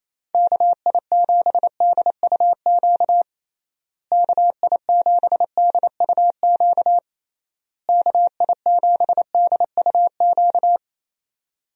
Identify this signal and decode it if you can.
Morse code, heard as a keyed tone, about 28 wpm, 700 Hz. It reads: KI7DUQ KI7DUQ KI7DUQ
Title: USA Call Signs - 6 Characters - Repeated 3x - 28wpm